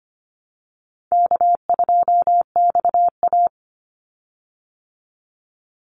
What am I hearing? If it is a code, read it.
Morse code, heard as a keyed tone, about 25 wpm, 700 Hz. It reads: K2XA
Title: Easy Call Signs Repeated 2x - 25wpm